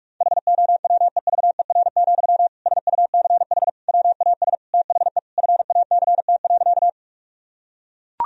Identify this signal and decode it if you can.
Morse code, heard as a keyed tone, about 41 wpm, 700 Hz. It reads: HOWEVER, SUCH WAS THE FACT.
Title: Sentences from Top 1000 Words - Review Part 3 - 45wpm